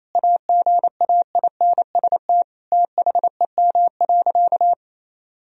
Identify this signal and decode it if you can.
Morse code, heard as a keyed tone, about 27 wpm, 700 Hz. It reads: AGAINST THEM.
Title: Sentences from Top 1000 Words - Review Part 2 - 28wpm